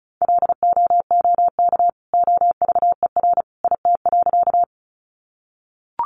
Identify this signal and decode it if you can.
Morse code, heard as a keyed tone, about 34 wpm, 700 Hz. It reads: LOOK OVER IT.
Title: Sentences from Top 100 Words 35wpm